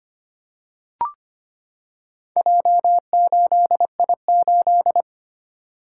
Morse code, heard as a keyed tone, about 25 wpm, 700 Hz. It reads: J8I8
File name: practice / 25wpm / Four Letter-Number Combinations - 25wpm